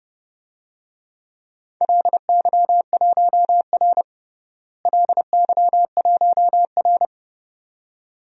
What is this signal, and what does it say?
Morse code, heard as a keyed tone, about 30 wpm, 700 Hz. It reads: LY1R LY1R
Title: DXCC-100 Call Signs - 4 Characters - Repeated 2x - 30wpm